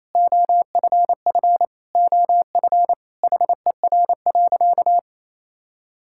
Morse code, heard as a keyed tone, about 28 wpm, 700 Hz. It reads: OFF OF HER.
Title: Sentences from Top 1000 Words - Review Part 5 - 28wpm